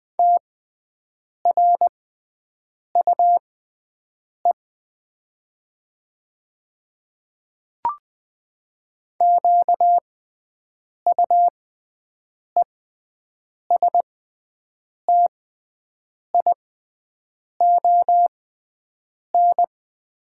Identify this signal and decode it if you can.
Morse code, heard as a keyed tone, about 20 wpm, spaced as for 7 wpm, 700 Hz. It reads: TRUE QUESTION
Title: 6x Character Spacing - Top 500 Words - 20wpm